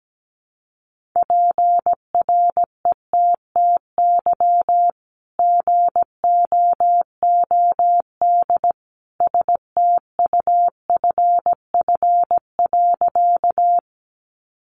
Morse code, heard as a keyed tone, about 17 wpm, 700 Hz. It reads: PRETTY GOOD STUFF.
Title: Sentences from Top 1400 Words 17wpm